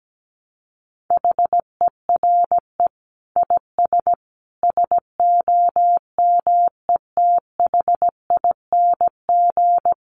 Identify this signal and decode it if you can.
Morse code, heard as a keyed tone, about 17 wpm, 700 Hz. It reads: HERE IS SOMETHING
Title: Sentences from Top 1000 Words 17wpm